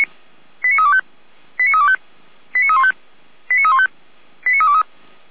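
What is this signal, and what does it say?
Five-tone selective call (ZVEI), heard as quick runs of five tones, tones of about 70 ms. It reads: 89325 89325 89315 89315 89323
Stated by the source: ZVEI five-tone call received by SDR.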